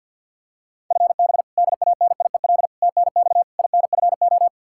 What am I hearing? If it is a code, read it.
Morse code, heard as a keyed tone, about 52 wpm, 700 Hz. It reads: FB DANIEL TNX INFO
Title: Practice QSO - Exchange 3 - More Info - 50wpm